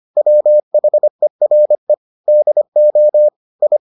Morse code, heard as a keyed tone, about 25 wpm, 600 Hz. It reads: WHERE DO I